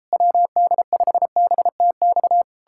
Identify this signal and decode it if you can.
Morse code, heard as a keyed tone, about 38 wpm, 700 Hz. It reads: WD5BTX